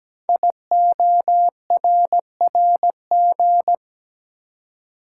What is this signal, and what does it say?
Morse code, heard as a keyed tone, about 17 wpm, 700 Hz. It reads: IORRG
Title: Code Groups - 5 Letters - 17wpm